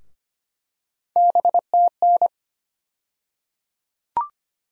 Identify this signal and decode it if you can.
Morse code, heard as a keyed tone, about 25 wpm, 700 Hz. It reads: BTN
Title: USA Call Sign Postfixes - 3 Letters 25wpm